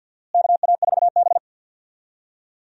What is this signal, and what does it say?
Morse code, heard as a keyed tone, about 52 wpm, 700 Hz. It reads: KA4B
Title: USA Call Signs - 4 Characters - Repeated 3x - 50wpm